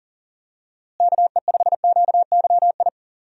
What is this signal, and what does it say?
Morse code, heard as a keyed tone, about 40 wpm, 700 Hz. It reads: KE5QYI
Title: USA Call Signs - 6 Characters - Repeated 2x - 40wpm